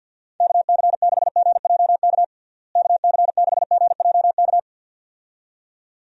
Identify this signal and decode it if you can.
Morse code, heard as a keyed tone, about 51 wpm, 700 Hz. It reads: KC6GJK KC6GJK
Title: USA Call Signs - 6 Characters - Repeated 2x - 50wpm